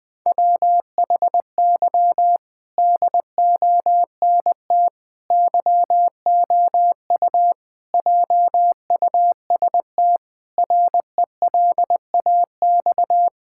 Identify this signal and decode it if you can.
Morse code, heard as a keyed tone, about 21 wpm, 700 Hz. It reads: WHY DONT YOU JUST RELAX
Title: Sentences with 5-Word Phrases - Spoken Beginning - 20wpm